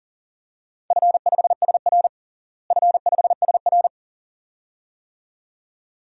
Morse code, heard as a keyed tone, about 40 wpm, 700 Hz. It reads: F5SR F5SR
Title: DXCC-100 Call Signs - 4 Characters - Repeated 2x - 40wpm